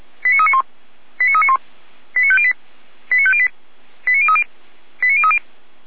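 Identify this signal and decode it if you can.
Five-tone selective call (ZVEI), heard as quick runs of five tones, tones of about 70 ms. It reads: 89391 89391 89508 89508 89030 89030
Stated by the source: ZVEI five-tone call 89125, received by SDR.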